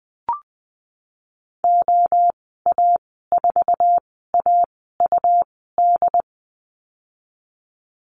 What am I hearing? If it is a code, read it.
Morse code, heard as a keyed tone, about 20 wpm, spaced as for 15 wpm, 700 Hz. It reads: OA4AUD
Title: 147 Lesson 24g - Letter y - Call Signs 20wpm